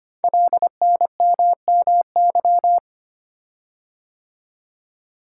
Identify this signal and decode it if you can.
Morse code, heard as a keyed tone, about 25 wpm, 700 Hz. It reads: LNMMY
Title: Code Groups - 5 Letters - 25wpm